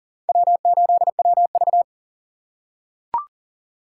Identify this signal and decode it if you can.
Morse code, heard as a keyed tone, about 40 wpm, 700 Hz. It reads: W8WV
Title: USA Call Signs - 4 Characters - 2x Character Spacing - 40wpm